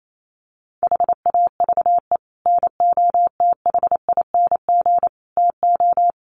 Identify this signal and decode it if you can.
Morse code, heard as a keyed tone, about 28 wpm, 700 Hz. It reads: HAVE NOTHING TO
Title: Sets of 5 Words 28wpm